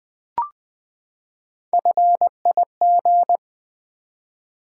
Morse code, heard as a keyed tone, about 20 wpm, 700 Hz. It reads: FIG
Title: Build A Word - 3 Letters At A Time - 8-Letter Words - 20wpm